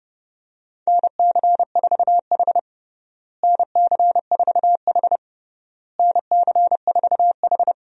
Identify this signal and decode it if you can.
Morse code, heard as a keyed tone, about 32 wpm, 700 Hz. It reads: NC4H NC4H NC4H
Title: USA Call Signs - 4 Characters - Repeated 3x - 30wpm